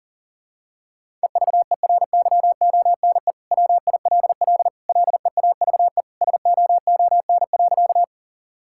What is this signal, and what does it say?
Morse code, heard as a keyed tone, about 40 wpm, 700 Hz. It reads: EVERYONE WILL LEAVE SOON.
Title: Sentences from Top 1000 Words - Review Part 5 - 40wpm